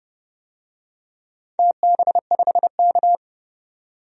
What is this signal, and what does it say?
Morse code, heard as a keyed tone, about 30 wpm, 700 Hz. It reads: TB5K
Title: DXCC-100 Call Signs - 4 Characters - 30wpm